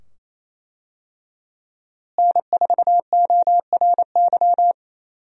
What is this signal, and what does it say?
Morse code, heard as a keyed tone, about 27 wpm, 700 Hz. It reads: N4ORY
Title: USA Call Signs - 5 Characters - Repeated 2x - 28wpm